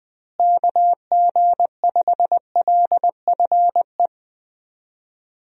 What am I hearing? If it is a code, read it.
Morse code, heard as a keyed tone, about 20 wpm, 700 Hz. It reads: KG5LFE
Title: USA Call Signs - 6 Characters - Repeated 3x - 20wpm